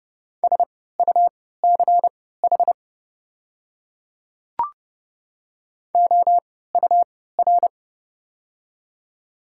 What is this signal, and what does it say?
Morse code, heard as a keyed tone, about 30 wpm, spaced as for 17 wpm, 700 Hz. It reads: SUCH OUR
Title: 3x Character Spacing - Top 300 Words - 30wpm